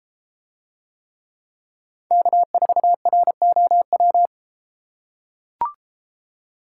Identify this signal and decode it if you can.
Morse code, heard as a keyed tone, about 33 wpm, 700 Hz. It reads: K4ROW